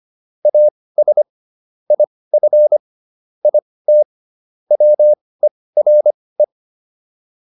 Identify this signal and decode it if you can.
Morse code, heard as a keyed tone, about 25 wpm, spaced as for 18 wpm, 600 Hz. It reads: AS IF IT WERE